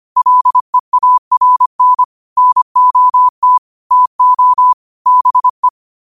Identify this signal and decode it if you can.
Morse code, heard as a keyed tone, about 25 wpm, 1000 Hz. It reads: LEARN NOT TO BE
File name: qotdfiles / 25wpm